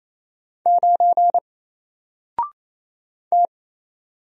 Morse code, heard as a keyed tone, about 27 wpm, 700 Hz. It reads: 9 T
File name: Single-Letter-Number-28wpm